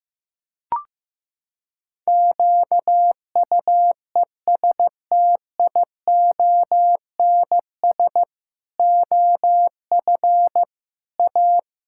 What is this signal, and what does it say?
Morse code, heard as a keyed tone, about 15 wpm, 700 Hz. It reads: QUESTIONS OF A
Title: Pangram Challenge v2 - 15wpm